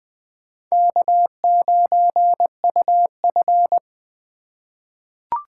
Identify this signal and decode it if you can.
Morse code, heard as a keyed tone, about 21 wpm, 700 Hz. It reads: K9UF
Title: USA Call Signs - 3x Character Spacing - 20wpm